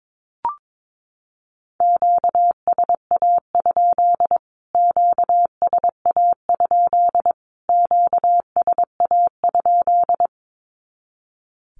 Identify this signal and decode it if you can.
Morse code, heard as a keyed tone, about 22 wpm, 700 Hz. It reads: QSA? QSA? QSA?
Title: Q-Codes Repeated 3x 22wpm